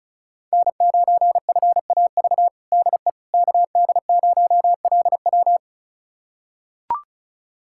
Morse code, heard as a keyed tone, about 34 wpm, 700 Hz. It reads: N9FAV DE KD0LW
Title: USA Callsign DE Callsign - 5 Characters - 35wpm